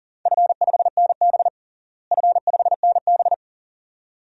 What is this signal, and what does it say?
Morse code, heard as a keyed tone, about 40 wpm, 700 Hz. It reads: F5NB F5NB
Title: DXCC-100 Call Signs - 4 Characters - Repeated 2x - 40wpm